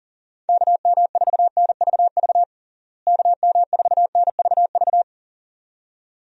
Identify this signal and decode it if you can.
Morse code, heard as a keyed tone, about 40 wpm, 700 Hz. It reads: KM4NVV KM4NVV
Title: USA Call Signs - 6 Characters - Repeated 2x - 40wpm